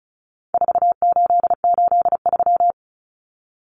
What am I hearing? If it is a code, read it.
Morse code, heard as a keyed tone, about 34 wpm, 700 Hz. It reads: 4883
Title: Numbers - 4-Digit 35wpm